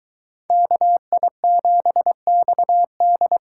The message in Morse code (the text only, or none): KI7XD